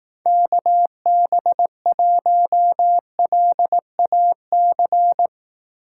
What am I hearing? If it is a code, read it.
Morse code, heard as a keyed tone, about 18 wpm, 700 Hz. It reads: KB1LAC
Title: Speed Racing - USA 2x3 Call Signs 15wpm